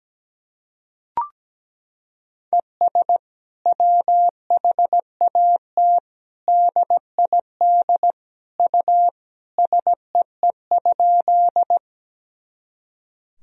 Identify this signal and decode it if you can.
Morse code, heard as a keyed tone, about 17 wpm, 700 Hz. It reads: ES WHAT DID U SEE?
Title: Sentences from Top 100 Words and Abbreviations 17wpm